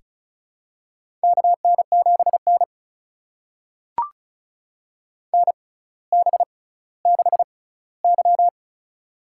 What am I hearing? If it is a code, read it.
Morse code, heard as a keyed tone, about 36 wpm, spaced as for 18 wpm, 700 Hz. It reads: KN7N NB6Y
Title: USA Call Signs - 6x Character Spacing - 35wpm